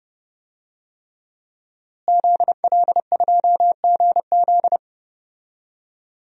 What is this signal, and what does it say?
Morse code, heard as a keyed tone, about 32 wpm, 700 Hz. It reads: ZL2GZ
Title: DXCC-100 Call Signs - 5 Characters - Repeated 3x - 30wpm